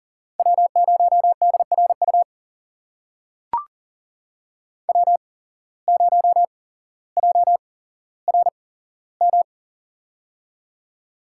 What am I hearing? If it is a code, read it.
Morse code, heard as a keyed tone, about 40 wpm, spaced as for 17 wpm, 700 Hz. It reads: W0DRU W0JRM
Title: USA Call Signs - 5 Characters - 8x Character Spacing - 40wpm